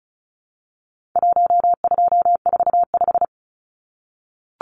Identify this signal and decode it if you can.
Morse code, heard as a keyed tone, about 34 wpm, 700 Hz. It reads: 1245
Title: Numbers - Contest Style - 4-Digit 35wpm